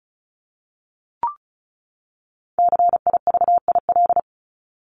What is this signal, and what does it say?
Morse code, heard as a keyed tone, about 34 wpm, 700 Hz. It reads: CIVIL
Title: Top 1751-2000 Words 35wpm